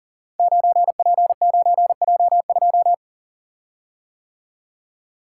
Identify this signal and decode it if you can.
Morse code, heard as a keyed tone, about 40 wpm, 700 Hz. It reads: 9P9J2
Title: Code Groups - 5 Letter:Numbers - 40wpm